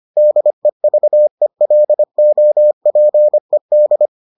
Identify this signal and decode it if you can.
Morse code, heard as a keyed tone, about 25 wpm, 600 Hz. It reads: DEVELOPED